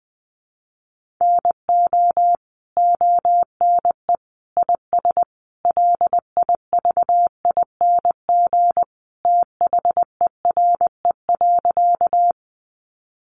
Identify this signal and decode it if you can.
Morse code, heard as a keyed tone, about 20 wpm, 700 Hz. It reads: NO ONE IS LIVING THERE.
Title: Sentences from Top 1100 Words 20wpm